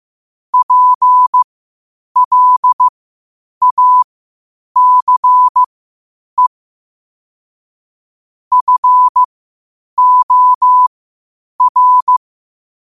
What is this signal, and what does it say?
Morse code, heard as a keyed tone, about 15 wpm, spaced as for 8 wpm, 1000 Hz. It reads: PLACE FOR
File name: 8wpm